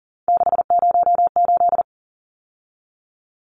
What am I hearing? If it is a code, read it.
Morse code, heard as a keyed tone, about 40 wpm, 700 Hz. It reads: 608
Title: Numbers - 3-Digit 40wpm